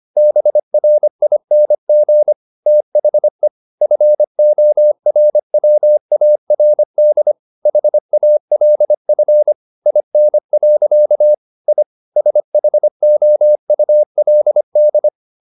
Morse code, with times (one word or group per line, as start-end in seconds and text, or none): 0.2-2.3 BRING
2.7-3.5 THE
3.8-7.3 FORWARD
7.7-9.5 HALF
9.9-11.3 IN.
11.7-11.8 I
12.2-15.1 SHOULD